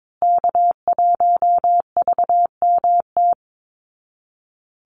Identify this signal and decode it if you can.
Morse code, heard as a keyed tone, about 22 wpm, 700 Hz. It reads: K1VMT
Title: Call Signs Repeated 2x - 22wpm